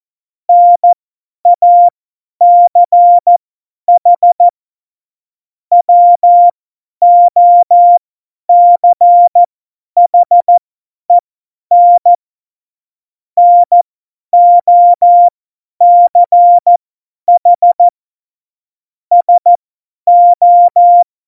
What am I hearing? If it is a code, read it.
Morse code, heard as a keyed tone, about 14 wpm, spaced as for 10 wpm, 700 Hz. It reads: NACH WOCHEN NOCH SO
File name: Morsekurs / Witze / Witz 29. Januar 2019, 50 F70